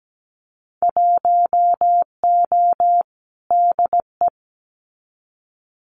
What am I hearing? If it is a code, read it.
Morse code, heard as a keyed tone, about 17 wpm, 700 Hz. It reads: 1O DE
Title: Winter Field Day Exchanges 17wpm